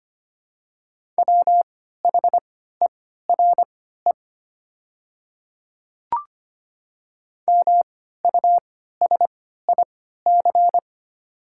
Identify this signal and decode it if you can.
Morse code, heard as a keyed tone, about 25 wpm, spaced as for 14 wpm, 700 Hz. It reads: WHERE MUSIC
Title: 3x Character Spacing - Top 500 Words - 25wpm